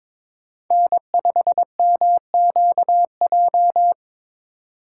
Morse code, heard as a keyed tone, about 22 wpm, 700 Hz. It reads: N5MQJ
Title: USA Call Signs - 5 Characters - 3x Character Spacing - 22wpm